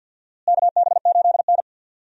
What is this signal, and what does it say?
Morse code, heard as a keyed tone, about 51 wpm, 700 Hz. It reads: KB8N
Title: USA Call Signs - 4 Characters - Repeated 3x - 50wpm